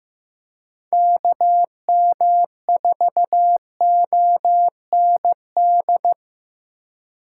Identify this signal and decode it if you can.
Morse code, heard as a keyed tone, about 15 wpm, 700 Hz. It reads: KM4OND
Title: USA Call Signs - 6 Characters - 4x Character Spacing - 15wpm